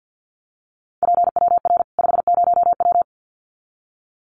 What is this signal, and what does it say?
Morse code, heard as a keyed tone, about 50 wpm, 700 Hz. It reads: PWR 50W
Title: Common QSO Phrases 50wpm